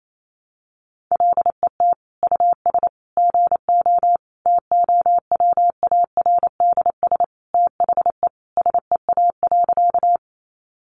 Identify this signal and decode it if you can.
Morse code, heard as a keyed tone, about 28 wpm, 700 Hz. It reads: LET US GO TOWARDS THE SEA.
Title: Sentences from Top 1100 Words 28wpm